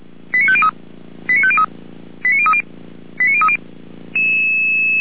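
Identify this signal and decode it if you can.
Five-tone selective call (ZVEI), heard as quick runs of five tones, tones of about 70 ms. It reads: 89503 89503 89030 89030
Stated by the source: ZVEI five-tone selective call received by SDR.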